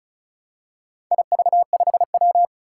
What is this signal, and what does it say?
Morse code, heard as a keyed tone, about 36 wpm, 690 Hz. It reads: IV5W
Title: Four Challenging Letter-Number Combinations 35wpm